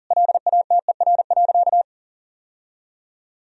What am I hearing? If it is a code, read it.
Morse code, heard as a keyed tone, about 40 wpm, 700 Hz. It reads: LATER.
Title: Sentences from Top 1000 Words - Review Part 4 - 40wpm